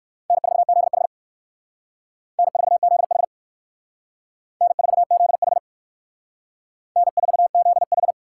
Morse code, heard as a keyed tone, about 56 wpm, 700 Hz. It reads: N4ZH N4ZH N4ZH N4ZH